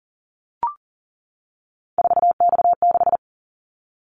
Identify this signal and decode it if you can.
Morse code, heard as a keyed tone, about 38 wpm, 700 Hz. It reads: 4X6
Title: Callsign Prefixes for Top 50 DXCC - 40wpm